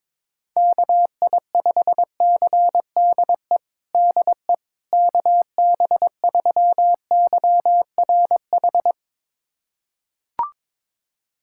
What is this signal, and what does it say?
Morse code, heard as a keyed tone, about 22 wpm, 700 Hz. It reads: KI5CDE DE KB3YRH
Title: USA Callsign DE Callsign 22wpm